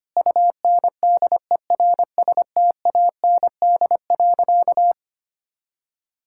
Morse code, heard as a keyed tone, about 25 wpm, 700 Hz. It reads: UNDERSTAND.
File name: practice / 25wpm / Sentences with 4-Word Phrases - 25wpm